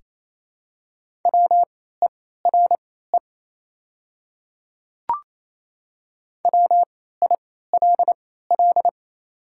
Morse code, heard as a keyed tone, about 28 wpm, spaced as for 16 wpm, 700 Hz. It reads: WERE WILL
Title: 3x Character Spacing - Top 100 Words - 28wpm